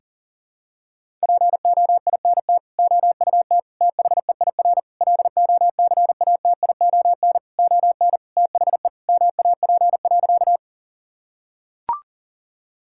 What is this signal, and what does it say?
Morse code, heard as a keyed tone, about 40 wpm, 700 Hz. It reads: POINT OUT THEIR LOCATION ON THE MAP.
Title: Sentences from Top 1000 Words - Review Part 6 - 40wpm